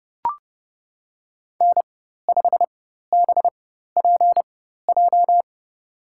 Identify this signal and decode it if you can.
Morse code, heard as a keyed tone, about 30 wpm, spaced as for 14 wpm, 700 Hz. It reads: N5BPJ
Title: USA Call Signs - 4x Character Spacing - 30wpm